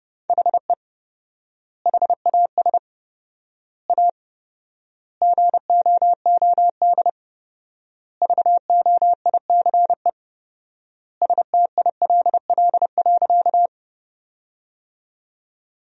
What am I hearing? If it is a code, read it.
Morse code, heard as a keyed tone, about 30 wpm, 700 Hz. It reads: HE HAS A GOOD VOICE STILL.
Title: 4x Word-Spacing - Sentences from Top 500 Words - 30wpm